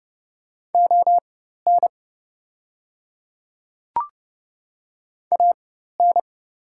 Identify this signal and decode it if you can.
Morse code, heard as a keyed tone, about 30 wpm, spaced as for 14 wpm, 690 Hz. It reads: ON AN
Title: 4x Character Spacing - Top 100 Words - 30wpm